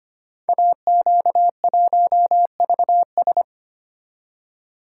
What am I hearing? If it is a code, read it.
Morse code, heard as a keyed tone, about 25 wpm, 700 Hz. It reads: AQ1VS
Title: Code Groups - 5 Letter:Numbers - 25wpm